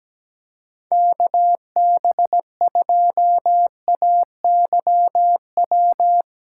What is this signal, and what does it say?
Morse code, heard as a keyed tone, about 17 wpm, 700 Hz. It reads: KB2AYW
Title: USA Callsign DE Callsign - 6 Characters - 17wpm